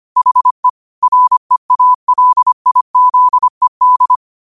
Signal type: Morse code